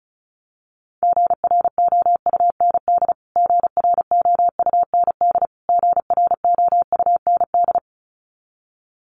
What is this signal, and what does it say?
Morse code, heard as a keyed tone, about 34 wpm, 700 Hz. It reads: GROUND GROUND GROUND
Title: Top 500 Words Repeated 35wpm